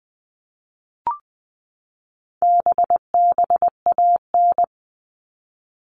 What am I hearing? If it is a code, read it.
Morse code, heard as a keyed tone, about 19 wpm, 700 Hz. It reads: BBAN
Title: All Cut Numbers - 4-Digit 20wpm